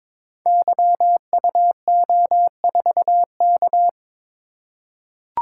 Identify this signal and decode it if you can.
Morse code, heard as a keyed tone, about 22 wpm, 700 Hz. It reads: YUO4K
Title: Code Groups - 5 Letter:Numbers - 22wpm